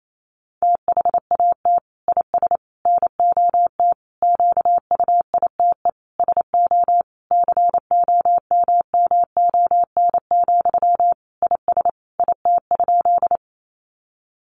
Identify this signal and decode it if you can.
Morse code, heard as a keyed tone, about 28 wpm, 700 Hz. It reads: THAT IS NOT QUITE SO COMMON, IS IT?
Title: Sentences from Top 700 Words 28wpm